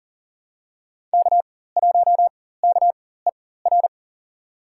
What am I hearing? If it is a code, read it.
Morse code, heard as a keyed tone, about 40 wpm, spaced as for 19 wpm, 700 Hz. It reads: K1KER
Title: USA Call Signs - 5 Characters - 4x Character Spacing - 40wpm